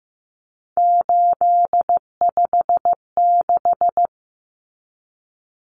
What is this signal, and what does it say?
Morse code, heard as a keyed tone, about 15 wpm, 700 Hz. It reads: 856
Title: Numbers - 3-Digit 15wpm